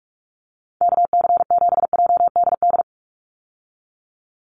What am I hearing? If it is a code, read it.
Morse code, heard as a keyed tone, about 46 wpm, 700 Hz. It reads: KC7JDD
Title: USA Call Signs 45wpm